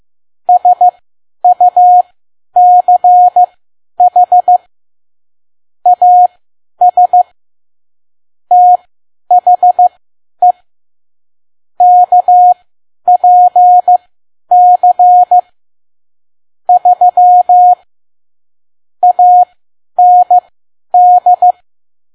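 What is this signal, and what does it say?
Morse code, heard as a keyed tone, about 15 wpm, spaced as for 10 wpm, 720 Hz. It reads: SUCH AS THE KPC 3 AND
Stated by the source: Morse code practice at 10 WPM.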